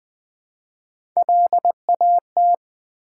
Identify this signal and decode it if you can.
Morse code, heard as a keyed tone, about 20 wpm, 700 Hz. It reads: LAT
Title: Build A Word - 1 Letter At A Time - 3 to 5-Letter Words - 20wpm